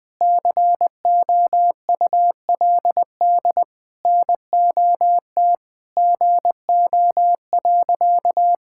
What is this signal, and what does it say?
Morse code, heard as a keyed tone, about 20 wpm, 700 Hz. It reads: COULD NOT GO.
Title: Sentences with 5-Word Phrases - 20wpm